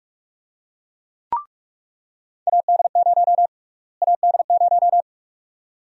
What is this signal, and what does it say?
Morse code, heard as a keyed tone, about 46 wpm, 700 Hz. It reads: AD0 AD0
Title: USA Call Sign Prefixes - Repeated 2x - 45wpm